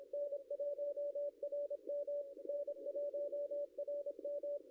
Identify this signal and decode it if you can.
Morse code, heard as a keyed tone, about 26 wpm, 560 Hz. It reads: N1RM N1RM
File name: contests / ss-cw-k5tr-op-n5rz -0700